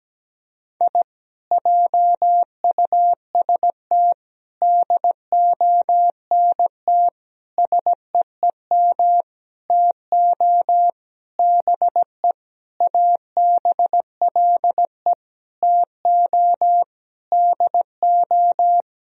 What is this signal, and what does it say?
Morse code, heard as a keyed tone, about 17 wpm, 700 Hz. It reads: I JUST DONT SEEM TO BE ABLE TO DO
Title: Sentences from Top 1000 Words - Review Part 2 - 17wpm